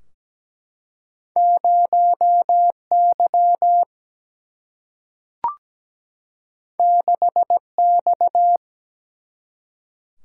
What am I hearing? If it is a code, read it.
Morse code, heard as a keyed tone, about 17 wpm, 700 Hz. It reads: 0Y 6X
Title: Single-Letter Single-Number 17wpm